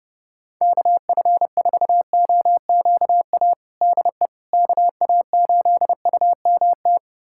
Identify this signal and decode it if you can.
Morse code, heard as a keyed tone, about 30 wpm, 700 Hz. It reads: KF4OQA DE KA8UMT
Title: USA Callsign DE Callsign - 6 Characters - 30wpm